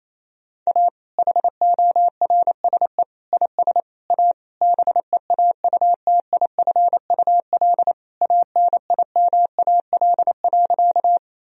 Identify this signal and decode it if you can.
Morse code, heard as a keyed tone, about 28 wpm, 700 Hz. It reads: A HORSE IS A BEAUTIFUL ANIMAL.
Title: Sentences from Top 1000 Words - Review Part 1 - 28wpm